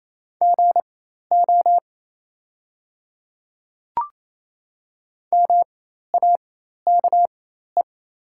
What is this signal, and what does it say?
Morse code, heard as a keyed tone, about 28 wpm, spaced as for 13 wpm, 690 Hz. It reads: GO MAKE
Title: 4x Character Spacing - Top 200 Words - 28wpm